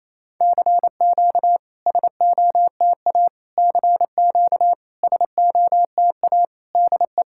Morse code, heard as a keyed tone, about 28 wpm, 700 Hz. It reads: CQ SOTA CQ SOTA DE
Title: SOTA QSO Practice 28wpm